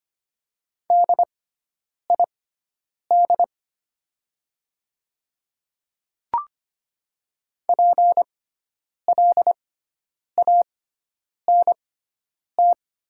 Morse code, heard as a keyed tone, about 25 wpm, spaced as for 9 wpm, 700 Hz. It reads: DID PLANT
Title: 6x Character Spacing - Top 300 Words - 25wpm